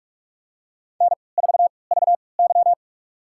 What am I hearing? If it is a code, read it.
Morse code, heard as a keyed tone, about 46 wpm, spaced as for 26 wpm, 700 Hz. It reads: N4VY
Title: USA Call Signs - 4 Characters - 3x Character Spacing - 45wpm